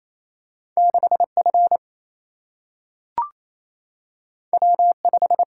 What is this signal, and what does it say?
Morse code, heard as a keyed tone, about 28 wpm, 700 Hz. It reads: 6F W5
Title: Single-Letter Single-Number 28wpm